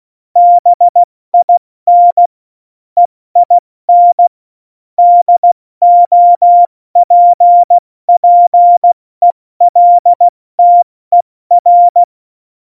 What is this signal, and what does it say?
Morse code, heard as a keyed tone, about 16 wpm, 700 Hz. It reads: BIN EIN DOPPELTER
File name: Morsekurs / Witze / Witz 03. November 2020, 70 F80